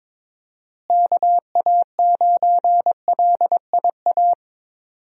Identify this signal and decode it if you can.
Morse code, heard as a keyed tone, about 22 wpm, 700 Hz. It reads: KA9LIA
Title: USA Call Signs - 6 Characters - Repeated 2x - 22wpm